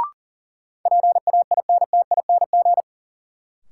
Morse code, heard as a keyed tone, about 40 wpm, 700 Hz. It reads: PAINTING
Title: Sentences from Top 1000 Words - Review Part 6 - New Word Drill - 40wpm